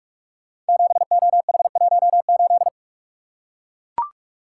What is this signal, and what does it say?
Morse code, heard as a keyed tone, about 44 wpm, 700 Hz. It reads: 7OH18
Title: Code Groups - 5 Letter:Numbers - 45wpm